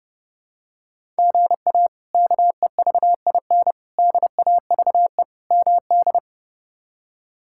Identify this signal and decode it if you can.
Morse code, heard as a keyed tone, about 30 wpm, 700 Hz. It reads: GA KEVIN DAVE MD
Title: K1USN SST QSO Practice - 30wpm